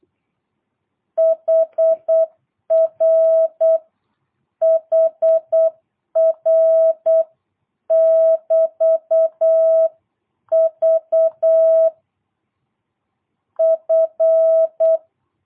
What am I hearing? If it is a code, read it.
Morse code, heard as a keyed tone, about 8 wpm, 660 Hz. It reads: HR HR <BT> V F
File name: jh8cbh / 2022cw